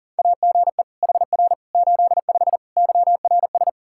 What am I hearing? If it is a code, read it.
Morse code, heard as a keyed tone, about 40 wpm, 700 Hz. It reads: AGE HR 85 YRS
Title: Practice QSO - Exchange 3 - More Info - 40wpm